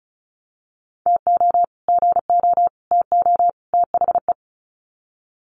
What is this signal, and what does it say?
Morse code, heard as a keyed tone, about 34 wpm, 700 Hz. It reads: TO GO TO THE
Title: Sets of 4 Words 35wpm